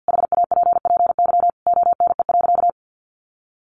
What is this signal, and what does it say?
Morse code, heard as a keyed tone, about 50 wpm, 700 Hz. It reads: HAPPY ONE.
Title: Sentences from Top 1400 Words 50wpm